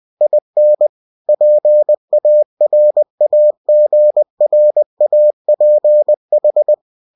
Morse code, heard as a keyed tone, about 20 wpm, 600 Hz. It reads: IN PARAGRAPH
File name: cw-books / winnie-the-pooh / winnie-20wpm-page-0116